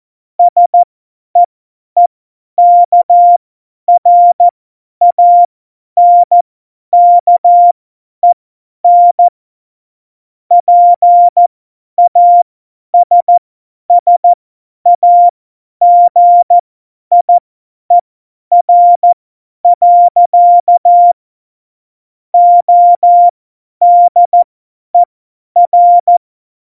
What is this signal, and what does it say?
Morse code, heard as a keyed tone, about 14 wpm, spaced as for 10 wpm, 700 Hz. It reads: SEEKRANKEN PASSAGIER. ODER
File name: Morsekurs / Witze / Witz 25. Februar 2020, 50 F70